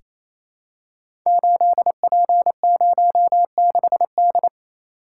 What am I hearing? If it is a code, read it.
Morse code, heard as a keyed tone, about 27 wpm, 700 Hz. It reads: 8P06D
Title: Code Groups - 5 Letter:Numbers - 28wpm